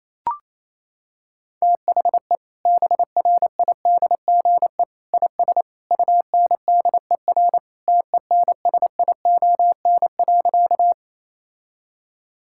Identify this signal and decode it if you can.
Morse code, heard as a keyed tone, about 28 wpm, 700 Hz. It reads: THE BRIDGE IS UNDER TENSION.
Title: Sentences from Top 1600 Words 28wpm